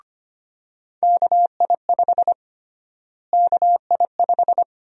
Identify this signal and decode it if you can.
Morse code, heard as a keyed tone, about 25 wpm, 700 Hz. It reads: KI5 KI5
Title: USA Call Sign Prefixes - Repeated 2x - 25wpm